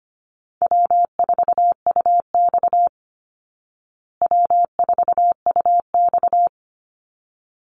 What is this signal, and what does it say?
Morse code, heard as a keyed tone, about 25 wpm, 700 Hz. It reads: W4UX W4UX
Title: Easy Call Signs Repeated 2x - 25wpm